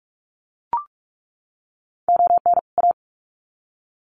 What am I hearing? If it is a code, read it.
Morse code, heard as a keyed tone, about 44 wpm, 700 Hz. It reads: ON A
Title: Sets of 2 Words 45wpm